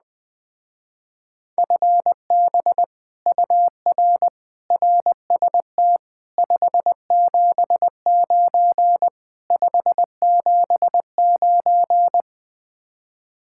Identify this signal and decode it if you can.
Morse code, heard as a keyed tone, about 20 wpm, 700 Hz. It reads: FB UR RST 579 579
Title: Speed-Racing Common QSO Phrases Encore - 17wpm